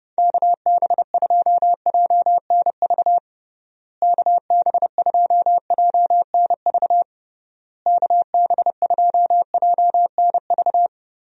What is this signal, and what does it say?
Morse code, heard as a keyed tone, about 30 wpm, 700 Hz. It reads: KB2JNV KB2JNV KB2JNV
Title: USA Call Signs - 6 Characters - Repeated 3x - 30wpm